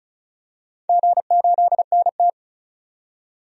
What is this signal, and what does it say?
Morse code, heard as a keyed tone, about 36 wpm, 700 Hz. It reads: G8NT
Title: DXCC-100 Call Signs - 4 Characters - 35wpm